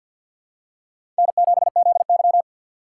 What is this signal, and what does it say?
Morse code, heard as a keyed tone, about 52 wpm, 700 Hz. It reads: N7ZY
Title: USA Call Signs - 4 Characters - 2x Character Spacing - 50wpm